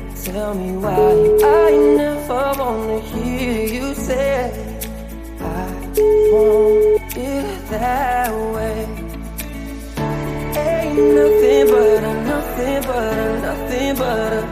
Ringback tone (Europe, 425 Hz), with noise.